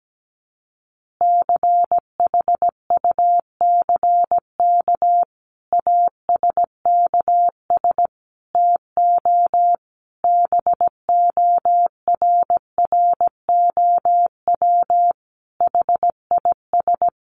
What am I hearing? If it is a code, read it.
Morse code, heard as a keyed tone, about 17 wpm, 700 Hz. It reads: CHUCK ASKS TO BORROW HIS
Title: Additional Easy Phrases 17wpm